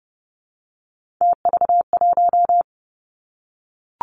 Morse code, heard as a keyed tone, about 30 wpm, 700 Hz. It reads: TV1
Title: Three Challenging Letter-Number Combinations 30wpm